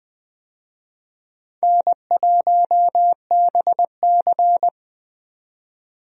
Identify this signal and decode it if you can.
Morse code, heard as a keyed tone, about 21 wpm, 700 Hz. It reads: N1BC
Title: USA Call Signs - 4 Characters - 20wpm